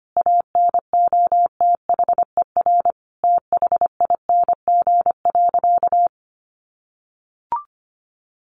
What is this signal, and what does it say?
Morse code, heard as a keyed tone, about 25 wpm, 700 Hz. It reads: ANOTHER THING.
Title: Sentences from Top 600 Words 25wpm